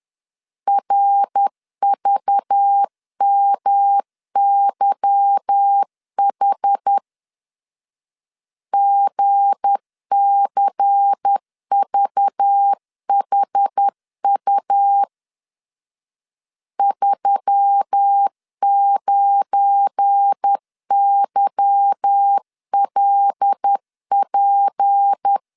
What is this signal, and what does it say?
Morse code, heard as a keyed tone, about 10 wpm, 800 Hz. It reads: RVMYH GCVHU 39YLP
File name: LEZIONI / 8 YJ45 / 8 Y 10wpm 2615